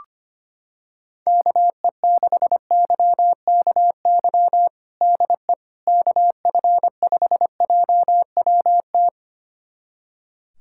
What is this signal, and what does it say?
Morse code, heard as a keyed tone, about 25 wpm, 700 Hz. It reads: KE6YKY DE KF5JWT
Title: USA Callsign DE Callsign - 6 Characters - 25wpm